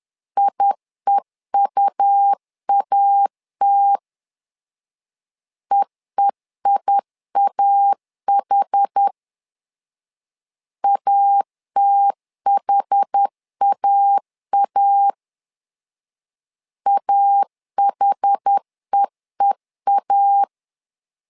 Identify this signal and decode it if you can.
Morse code, heard as a keyed tone, about 10 wpm, 800 Hz. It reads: IEUAT EEIAH ATHAA AHEEA
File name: LEZIONI / 2 UHDN / 2 UH 10wpm 1714